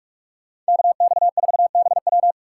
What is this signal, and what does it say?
Morse code, heard as a keyed tone, about 58 wpm, spaced as for 51 wpm, 700 Hz. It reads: KX4BW
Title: Speed Racing - USA Call Signs 45wpm